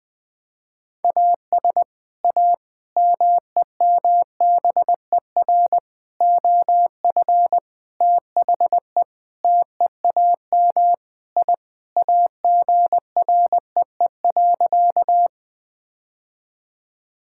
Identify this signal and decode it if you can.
Morse code, heard as a keyed tone, about 21 wpm, 700 Hz. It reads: AS A MEMBER OF THE TEAM I AGREE.
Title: Sentences with 5-Word Phrases - 20wpm